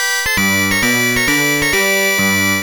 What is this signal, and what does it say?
There is no code.